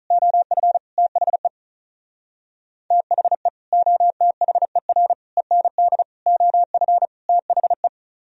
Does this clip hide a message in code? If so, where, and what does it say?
Yes: Morse code from 0.1 to 7.9 s, heard as a keyed tone, about 45 wpm, 700 Hz. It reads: OF THE THE OTHER END OF THE